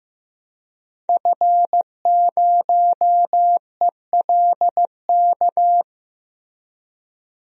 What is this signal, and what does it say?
Morse code, heard as a keyed tone, about 15 wpm, 700 Hz. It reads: F0ELK
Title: DXCC-100 Call Signs - 5 Characters - Repeated 2x - 15wpm